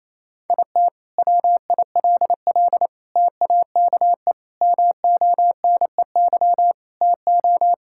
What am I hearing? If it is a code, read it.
Morse code, heard as a keyed tone, about 28 wpm, 700 Hz. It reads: IT WILL TAKE MONEY TO
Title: Sentences from Top 1600 Words 28wpm